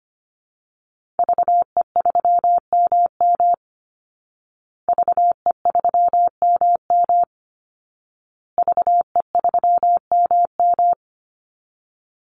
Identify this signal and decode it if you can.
Morse code, heard as a keyed tone, about 25 wpm, 700 Hz. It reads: VE3MM VE3MM VE3MM
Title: Call Signs Repeated 2x - 25wpm